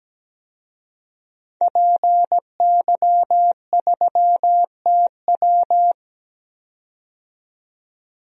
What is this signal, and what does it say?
Morse code, heard as a keyed tone, about 17 wpm, 700 Hz. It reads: PY3TW
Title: DXCC-100 Call Signs - 5 Characters - 17wpm